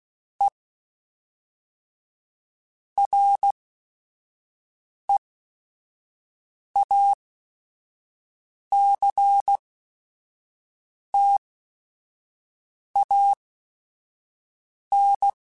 Morse code, heard as a keyed tone, about 16 wpm, spaced as for 5 wpm, 800 Hz. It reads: E REACTAN
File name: Technical Subjects / Technical Subjects-mp3s / XC @5